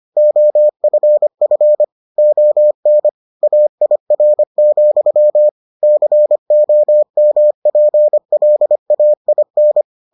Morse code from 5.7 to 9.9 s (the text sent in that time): COMPLAIN